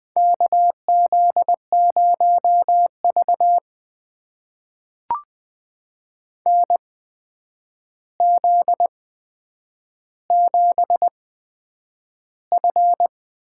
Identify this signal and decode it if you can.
Morse code, heard as a keyed tone, about 20 wpm, spaced as for 9 wpm, 700 Hz. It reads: KZ0V NZ7F
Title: USA Call Signs - 4 Characters - 8x Character Spacing - 20wpm